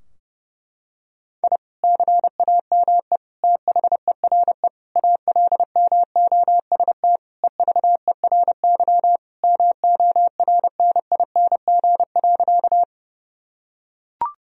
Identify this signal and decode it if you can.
Morse code, heard as a keyed tone, about 30 wpm, 700 Hz. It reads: I CAME THERE ALMOST EVERY MORNING.
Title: Sentences from Top 1000 Words - Review Part 4 - 30wpm